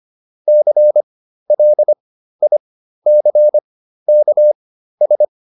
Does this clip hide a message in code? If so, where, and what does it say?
Yes: Morse code from 0.5 to 5.2 s, heard as a keyed tone, about 25 wpm, spaced as for 13 wpm, 600 Hz. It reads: CLICKS